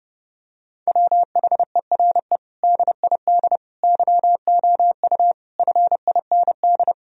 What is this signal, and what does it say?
Morse code, heard as a keyed tone, about 30 wpm, 700 Hz. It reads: WHERE DID YOU FIND
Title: Sentences from Top 1000 Words - Review Part 2 - 30wpm